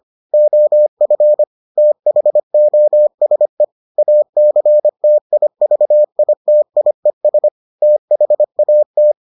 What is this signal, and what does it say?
Morse code, heard as a keyed tone, about 25 wpm, 600 Hz. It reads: OF THOSE ACTIVITIES THAT